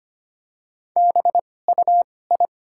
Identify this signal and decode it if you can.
Morse code, heard as a keyed tone, about 25 wpm, spaced as for 18 wpm, 700 Hz. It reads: BUI
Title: 2x Character Spacing - Top 300 Words - 25wpm